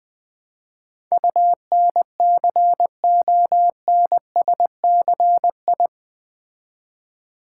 Morse code, heard as a keyed tone, about 20 wpm, 700 Hz. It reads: UNCONSCI
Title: Build A Word - 4 Letters At A Time - 11-Letter Words - 20wpm